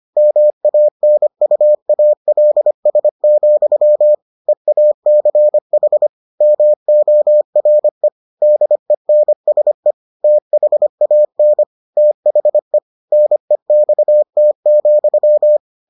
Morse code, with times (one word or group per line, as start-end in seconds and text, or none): 0.2-4.1 MANUALS,
4.5-6.1 EACH
6.4-8.1 MORE
8.4-9.9 DENSE
10.2-11.6 THAN
12.0-12.8 THE
13.1-15.6 NEXT,